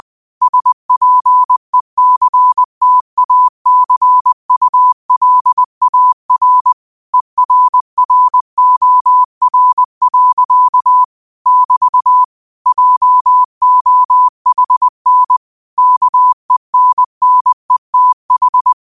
Morse code, heard as a keyed tone, about 20 wpm, 1000 Hz. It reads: SPECTACULAR ERROR. <BT> JOHN KENNETH
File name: qotdfiles / 20wpm